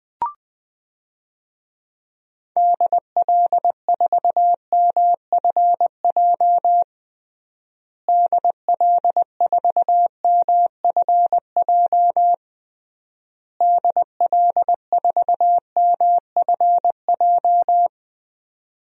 Morse code, heard as a keyed tone, about 20 wpm, 700 Hz. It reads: DL4MFJ DL4MFJ DL4MFJ
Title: DXCC-100 Call Signs - 6 Characters - Repeated 3x - 20wpm